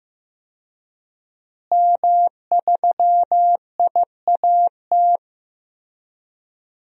Morse code, heard as a keyed tone, about 15 wpm, 700 Hz. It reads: M3IAT
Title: DXCC-100 Call Signs - 5 Characters - Repeated 2x - 15wpm